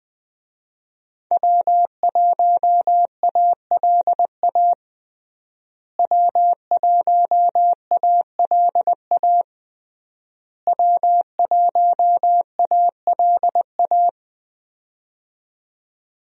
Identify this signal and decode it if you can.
Morse code, heard as a keyed tone, about 21 wpm, 700 Hz. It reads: W1ALA W1ALA W1ALA
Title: USA Call Signs - 5 Characters - Repeated 3x - 20wpm